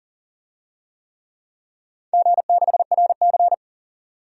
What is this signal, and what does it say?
Morse code, heard as a keyed tone, about 40 wpm, 700 Hz. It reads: G6RC
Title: DXCC-100 Call Signs - 4 Characters - 40wpm